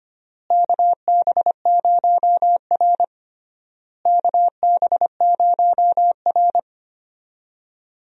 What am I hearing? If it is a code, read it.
Morse code, heard as a keyed tone, about 25 wpm, 700 Hz. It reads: KB0R KB0R
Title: USA Call Signs - 4 Characters - Repeated 3x - 25wpm